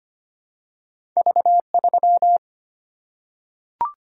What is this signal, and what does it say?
Morse code, heard as a keyed tone, about 25 wpm, 700 Hz. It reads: V3
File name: Two Letter-Number Combinations - 25wpm